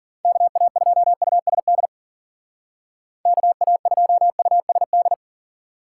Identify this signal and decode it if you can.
Morse code, heard as a keyed tone, about 50 wpm, 700 Hz. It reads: KA2USD KA2USD
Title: Speed Racing - USA 2x3 Call Signs 40wpm